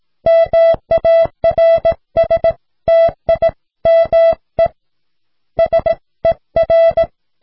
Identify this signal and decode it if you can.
Morse code, heard as a keyed tone, about 17 wpm, 660 Hz. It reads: MARSTIME SER